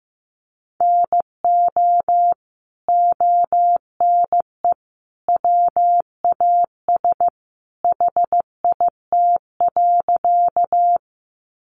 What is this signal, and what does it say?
Morse code, heard as a keyed tone, about 15 wpm, 700 Hz. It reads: NO ONE WAS HIT.
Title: Sentences from Top 1200 Words 15wpm